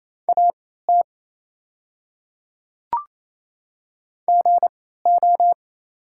Morse code, heard as a keyed tone, about 28 wpm, spaced as for 16 wpm, 700 Hz. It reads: AT GO
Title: 3x Character Spacing - Top 100 Words - 28wpm